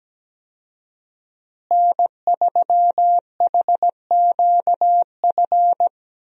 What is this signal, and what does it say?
Morse code, heard as a keyed tone, about 17 wpm, 700 Hz. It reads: N3HQF
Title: USA Call Signs - 5 Characters - Repeated 2x - 17wpm